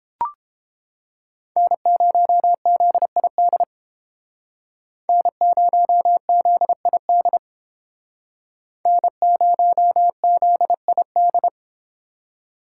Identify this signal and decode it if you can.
Morse code, heard as a keyed tone, about 31 wpm, 700 Hz. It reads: N0ZID N0ZID N0ZID